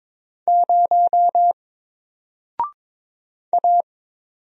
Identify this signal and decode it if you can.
Morse code, heard as a keyed tone, about 22 wpm, 700 Hz. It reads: 0 A
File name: Single-Letter-Number-22wpm